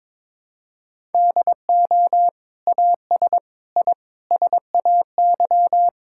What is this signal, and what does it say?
Morse code, heard as a keyed tone, about 22 wpm, 700 Hz. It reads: DO AS I SAY
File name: Sentences with 7-Word Phrases - 22wpm